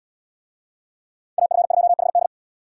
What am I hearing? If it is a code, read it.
Morse code, heard as a keyed tone, about 77 wpm, 700 Hz. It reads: AF2HD